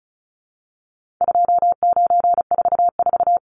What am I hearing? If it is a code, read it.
Morse code, heard as a keyed tone, about 34 wpm, 700 Hz. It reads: 2944
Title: Numbers - 4-Digit 35wpm